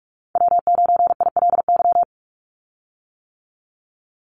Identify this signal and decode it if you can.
Morse code, heard as a keyed tone, about 46 wpm, 700 Hz. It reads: W9ILY
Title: Call Signs - 45wpm